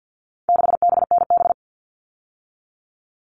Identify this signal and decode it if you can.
Morse code, heard as a keyed tone, about 50 wpm, 700 Hz. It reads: 6BNB
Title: All Cut Numbers - 4-Digit 50wpm